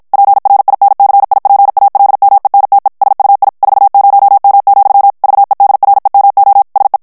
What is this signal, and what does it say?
Morse code, heard as a keyed tone, about 52 wpm, 800 Hz. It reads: PRINCIPALMENTE SUI 40M, VEDREMO SE
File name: morse mp3 / ITALIEN texte clair 1 vitesse 50 mots minute